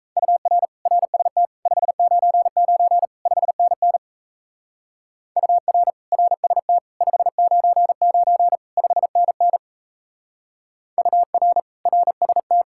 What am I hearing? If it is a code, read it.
Morse code, heard as a keyed tone, about 35 wpm, 700 Hz. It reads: UR RST 599 5NN UR RST 599 5NN UR RST
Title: Speed-Racing Common QSO Phrases Encore - 28wpm